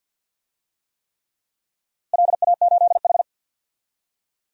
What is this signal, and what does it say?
Morse code, heard as a keyed tone, about 52 wpm, 700 Hz. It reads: LA8H
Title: DXCC-100 Call Signs - 4 Characters - Repeated 3x - 50wpm